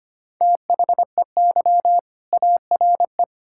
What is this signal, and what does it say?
Morse code, heard as a keyed tone, about 25 wpm, 700 Hz. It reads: THEY ARE
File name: practice / 25wpm / Sentences with 4-Word Phrases - 25wpm